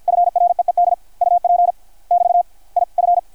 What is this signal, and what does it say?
Morse code, heard as a keyed tone, about 53 wpm, 660 Hz. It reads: FREED UP <BT> I F
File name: archived / CRSnet